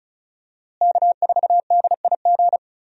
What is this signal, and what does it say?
Morse code, heard as a keyed tone, about 34 wpm, 700 Hz. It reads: K4DIG
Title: USA Call Signs - 5 Characters - Repeated 2x - 35wpm